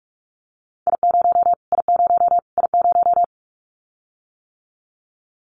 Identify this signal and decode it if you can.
Morse code, heard as a keyed tone, about 44 wpm, 700 Hz. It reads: I0 I0 I0
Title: Callsign Prefixes for Top 25 DXCC - Repeated 3x - 45wpm